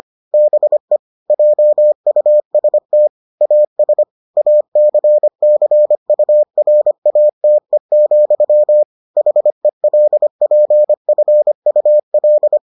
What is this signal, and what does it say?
Morse code, heard as a keyed tone, about 25 wpm, 600 Hz. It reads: BE JUST AS ACCURATE, HELPFUL